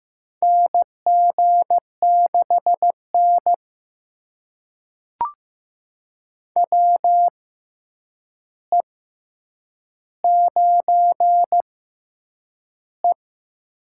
Morse code, heard as a keyed tone, about 15 wpm, spaced as for 8 wpm, 700 Hz. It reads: NG6N WE9E
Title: USA Call Signs - 4 Characters - 6x Character Spacing - 15wpm